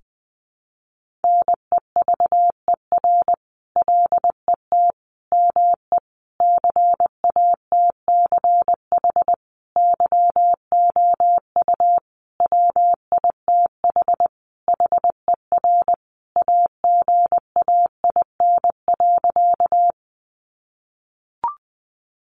Morse code, heard as a keyed tone, about 20 wpm, 700 Hz. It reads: NEVER LET ME CATCH YOU WITH HER AGAIN.
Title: Sentences from Top 1500 Words 20wpm